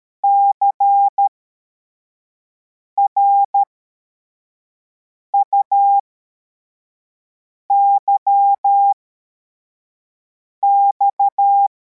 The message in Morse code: CRUYX